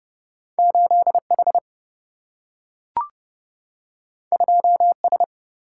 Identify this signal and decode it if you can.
Morse code, heard as a keyed tone, about 30 wpm, 700 Hz. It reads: 8 H 2 S
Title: Single-Letter Single-Number 30wpm